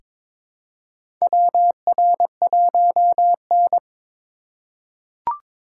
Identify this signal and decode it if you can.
Morse code, heard as a keyed tone, about 23 wpm, 700 Hz. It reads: WR1N
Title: USA Call Signs - 4 Characters - 3x Character Spacing - 22wpm